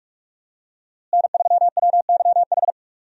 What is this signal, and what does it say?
Morse code, heard as a keyed tone, about 41 wpm, 700 Hz. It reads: N3WYH
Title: USA Call Signs - 5 Characters - Repeated 3x - 45wpm